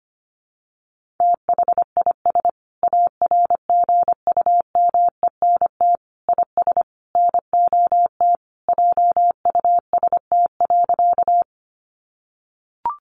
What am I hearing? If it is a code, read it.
Morse code, heard as a keyed tone, about 25 wpm, 700 Hz. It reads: THIS ARGUMENT IS NOT JUST.
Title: Sentences from Top 1400 Words 25wpm